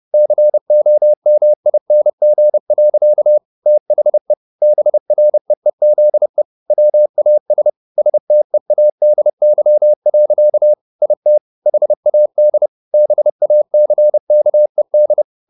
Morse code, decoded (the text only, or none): COMING. THE BREEZE WAS STEADY. IT HAD BACKED